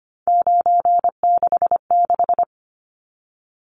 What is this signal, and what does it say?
Morse code, heard as a keyed tone, about 25 wpm, 700 Hz. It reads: 966
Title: Numbers - 3-Digit 25wpm